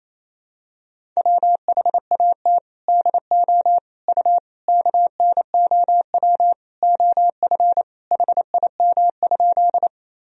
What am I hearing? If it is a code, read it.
Morse code, heard as a keyed tone, about 28 wpm, 700 Hz. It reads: WHAT DO U KNOW OF HIM?
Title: Sentences from Top 100 Words and Abbreviations 28wpm